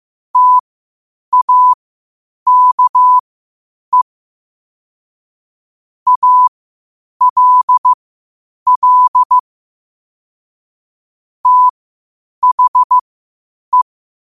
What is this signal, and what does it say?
Morse code, heard as a keyed tone, about 15 wpm, spaced as for 8 wpm, 1000 Hz. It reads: TAKE ALL THE
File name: qotdfiles / 8wpm